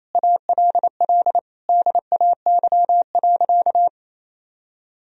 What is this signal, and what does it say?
Morse code, heard as a keyed tone, about 28 wpm, 700 Hz. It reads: ALL DAY.
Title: Sentences from Top 1000 Words - Review Part 3 - 28wpm